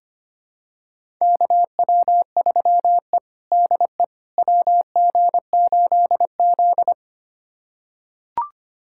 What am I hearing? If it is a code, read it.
Morse code, heard as a keyed tone, about 25 wpm, 700 Hz. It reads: KW3E DE WG8Z
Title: USA Callsign DE Callsign - 4 Characters - 25wpm